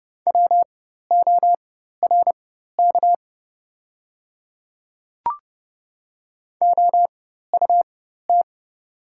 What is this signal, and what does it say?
Morse code, heard as a keyed tone, about 31 wpm, spaced as for 14 wpm, 700 Hz. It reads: WORK OUT
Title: 4x Character Spacing - Top 300 Words - 30wpm